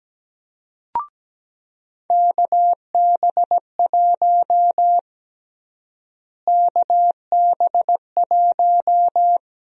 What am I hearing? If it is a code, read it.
Morse code, heard as a keyed tone, about 17 wpm, 700 Hz. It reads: KB1 KB1
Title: USA Call Sign Prefixes - Repeated 2x - 17wpm